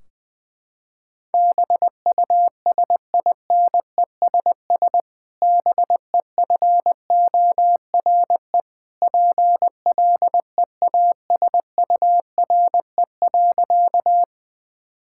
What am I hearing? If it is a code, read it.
Morse code, heard as a keyed tone, about 20 wpm, 700 Hz. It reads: BUSINESS BEFORE PLEASURE.
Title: Sentences from Top 1600 Words 20wpm